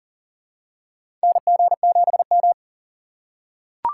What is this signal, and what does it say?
Morse code, heard as a keyed tone, about 43 wpm, 700 Hz. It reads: NG7M
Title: USA Call Signs - 4 Characters - 2x Character Spacing - 40wpm